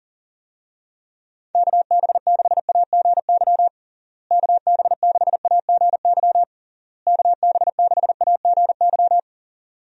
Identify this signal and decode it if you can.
Morse code, heard as a keyed tone, about 40 wpm, 700 Hz. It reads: KB6AGY KB6AGY KB6AGY
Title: USA Call Signs - 6 Characters - Repeated 3x - 40wpm